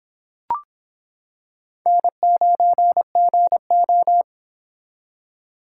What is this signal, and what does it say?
Morse code, heard as a keyed tone, about 26 wpm, 700 Hz. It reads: N9GO